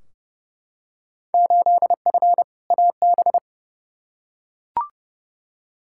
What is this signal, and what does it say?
Morse code, heard as a keyed tone, about 30 wpm, 700 Hz. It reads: 8F AB
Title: Field Day Exchanges Encore - 30wpm